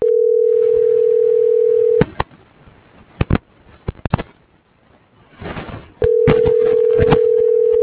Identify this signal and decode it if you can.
Ringback tone (North America), 440 + 480 Hz.